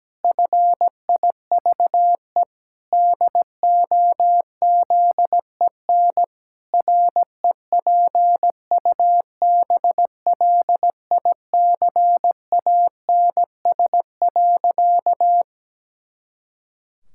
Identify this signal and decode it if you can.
Morse code, heard as a keyed tone, about 17 wpm, 700 Hz. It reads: FIVE DOZEN REPUBLICANS.
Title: Pangram Challenge v2 - 17wpm